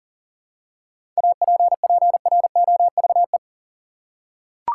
Morse code, heard as a keyed tone, about 40 wpm, 700 Hz. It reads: APPROVE
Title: Seven Letter Words - 40wpm